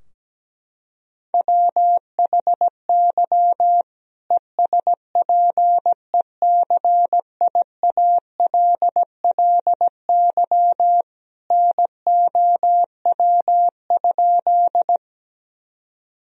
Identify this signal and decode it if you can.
Morse code, heard as a keyed tone, about 17 wpm, 700 Hz. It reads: WHY ESPECIALLY NOW?
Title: Sentences from Top 1000 Words - Review Part 4 - 17wpm